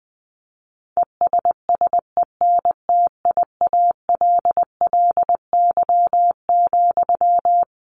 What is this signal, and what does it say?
Morse code, heard as a keyed tone, about 20 wpm, 700 Hz. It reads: ESSENTIALLY,
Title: Sentences from Top 1300 Words 20wpm